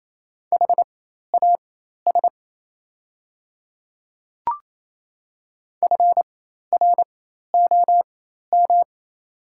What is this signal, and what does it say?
Morse code, heard as a keyed tone, about 28 wpm, spaced as for 13 wpm, 690 Hz. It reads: HAS FROM
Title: 4x Character Spacing - Top 100 Words - 28wpm